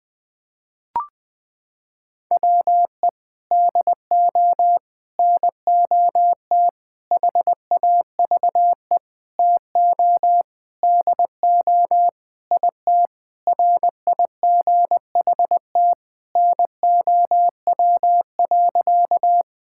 Morse code, heard as a keyed tone, about 20 wpm, 700 Hz. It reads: WE DO NOT HAVE TO DO IT RIGHT NOW.
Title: Sentences with 5-Word Phrases - 20wpm